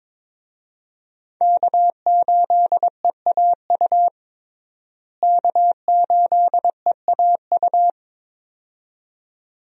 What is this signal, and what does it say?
Morse code, heard as a keyed tone, about 21 wpm, 700 Hz. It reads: K8EAU K8EAU
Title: USA Call Signs - 5 Characters - Repeated 2x - 22wpm